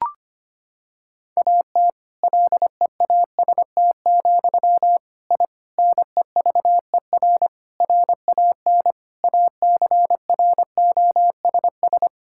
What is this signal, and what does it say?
Morse code, heard as a keyed tone, about 25 wpm, 700 Hz. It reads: AT LEAST, I NEVER RAN ACROSS
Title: Sentences from Top 1000 Words - Review Part 5 - 25wpm